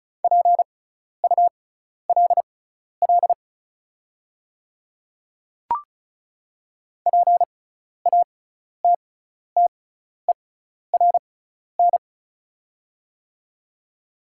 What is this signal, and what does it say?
Morse code, heard as a keyed tone, about 36 wpm, spaced as for 12 wpm, 690 Hz. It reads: PULL PATTERN
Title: 6x Character Spacing - Top 500 Words - 35wpm